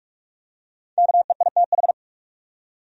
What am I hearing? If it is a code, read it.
Morse code, heard as a keyed tone, about 44 wpm, 700 Hz. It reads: KEITH
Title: USA Common Names - Max 5 Letters - 45wpm